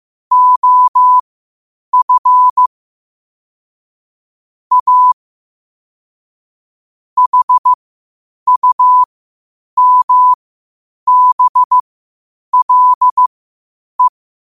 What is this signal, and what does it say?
Morse code, heard as a keyed tone, about 15 wpm, spaced as for 8 wpm, 1000 Hz. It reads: OF A HUMBLE
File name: qotdfiles / 8wpm